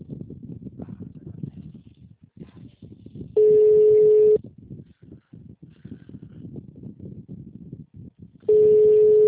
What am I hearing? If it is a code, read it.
Ringback tone (Europe, 425 Hz), with speech.